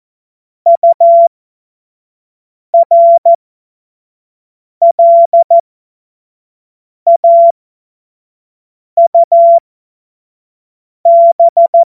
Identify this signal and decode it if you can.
Morse code, heard as a keyed tone, about 14 wpm, spaced as for 5 wpm, 680 Hz. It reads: URLAUB
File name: Morsekurs / Witze / Witz 12 März 2024 25 F70